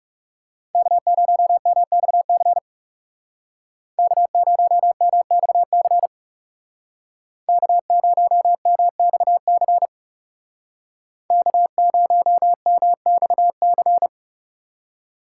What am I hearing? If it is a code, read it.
Morse code, heard as a keyed tone, about 36 wpm, 700 Hz. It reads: K0MXC K0MXC K0MXC K0MXC
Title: Speed Racing - USA 1x3 Call Signs 30wpm